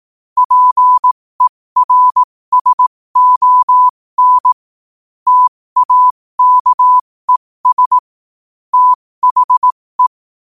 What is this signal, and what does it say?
Morse code, heard as a keyed tone, about 18 wpm, spaced as for 15 wpm, 1000 Hz. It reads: PERSON TAKES THE
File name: qotdfiles / 15wpm